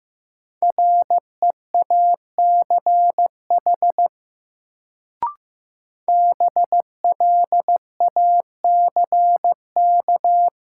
Morse code, heard as a keyed tone, about 15 wpm, 700 Hz. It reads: REACH BLACK
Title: Sentences from Top 400 Words - New Word Drill - 15wpm